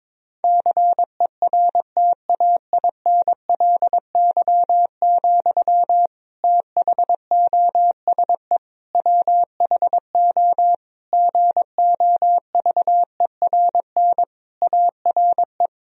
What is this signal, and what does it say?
Morse code, heard as a keyed tone, about 22 wpm, 700 Hz. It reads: CERTAINLY, THOSE WHO GOVERN ARE
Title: Sentences from Top 1000 Words - Review Part 2 - 22wpm